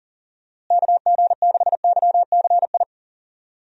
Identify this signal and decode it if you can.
Morse code, heard as a keyed tone, about 39 wpm, 700 Hz. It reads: KG6YCI
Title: USA Call Signs - 6 Characters - Repeated 3x - 40wpm